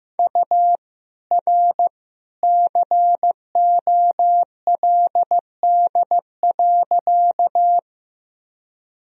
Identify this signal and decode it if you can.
Morse code, heard as a keyed tone, about 15 wpm, 700 Hz. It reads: U R COLD.
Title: Sentences from Top 400 Words and Abbreviations 15wpm